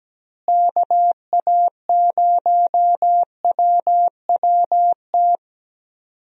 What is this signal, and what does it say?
Morse code, heard as a keyed tone, about 17 wpm, 700 Hz. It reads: KA0WWT
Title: USA Call Signs - 6 Characters - Repeated 3x - 17wpm